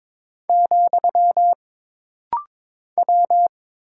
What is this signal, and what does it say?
Morse code, heard as a keyed tone, about 22 wpm, 700 Hz. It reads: , W
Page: This rapid-fire timing is geared to push your ICR (Instant Character Recognition) skills.